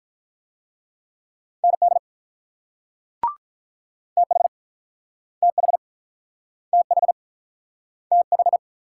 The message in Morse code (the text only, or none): ND TH TH TH TH